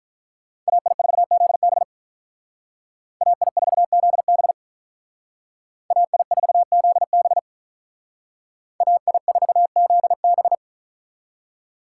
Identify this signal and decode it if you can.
Morse code, heard as a keyed tone, about 45 wpm, 700 Hz. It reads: AI4ZB AI4ZB AI4ZB AI4ZB